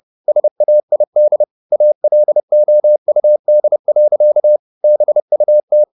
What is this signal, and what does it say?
Morse code, heard as a keyed tone, about 30 wpm, 600 Hz. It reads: SAID ALOUD. BUT
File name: the old man and the sea / the old man and the sea 1 30wpm